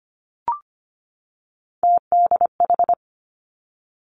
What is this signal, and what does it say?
Morse code, heard as a keyed tone, about 25 wpm, 700 Hz. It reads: TDH
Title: Three Challenging Letter-Number Combinations 25wpm